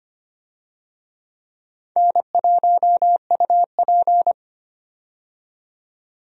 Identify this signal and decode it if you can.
Morse code, heard as a keyed tone, about 25 wpm, 700 Hz. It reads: N1UP
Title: USA Call Signs - 4 Characters - 25wpm